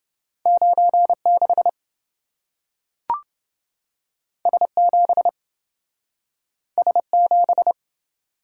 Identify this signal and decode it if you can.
Morse code, heard as a keyed tone, about 27 wpm, 700 Hz. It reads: 96 S7 S7